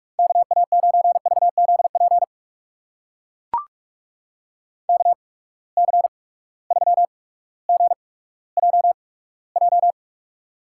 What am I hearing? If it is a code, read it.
Morse code, heard as a keyed tone, about 44 wpm, spaced as for 19 wpm, 700 Hz. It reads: KA9VZP KC3GJJ
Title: USA Call Signs - 6 Characters - 8x Character Spacing - 45wpm